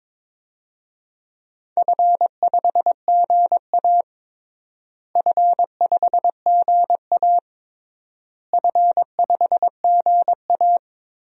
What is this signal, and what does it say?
Morse code, heard as a keyed tone, about 22 wpm, 700 Hz. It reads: F5GA F5GA F5GA
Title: DXCC-100 Call Signs - 4 Characters - Repeated 3x - 22wpm